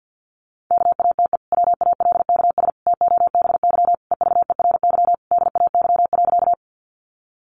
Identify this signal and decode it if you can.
Morse code, heard as a keyed tone, about 50 wpm, 700 Hz. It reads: KATE WALKS TOBY EVERY DAY.
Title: Easy Phrases 50wpm